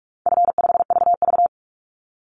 Morse code, heard as a keyed tone, about 46 wpm, 700 Hz. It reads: F5VV
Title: Call Signs - 45wpm